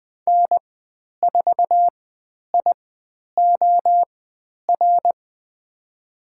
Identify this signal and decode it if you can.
Morse code, heard as a keyed tone, about 20 wpm, spaced as for 10 wpm, 700 Hz. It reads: N4IOR
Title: 102 Lesson 17g - Letter u - Call Signs 20wpm